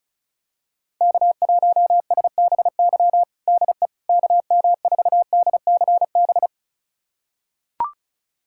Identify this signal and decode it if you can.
Morse code, heard as a keyed tone, about 36 wpm, 700 Hz. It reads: K1SBY DE KM4DCB
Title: USA Callsign DE Callsign 35wpm